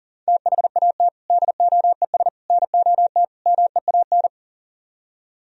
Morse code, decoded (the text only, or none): THAT DOES NOT MEAN